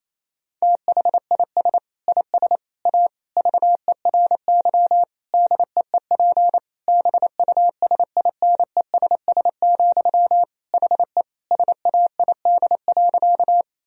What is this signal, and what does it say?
Morse code, heard as a keyed tone, about 27 wpm, 700 Hz. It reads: THIS IS A VERY DEEP BUSINESS, HE SAID.
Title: Sentences from Top 1000 Words - Review Part 3 - 28wpm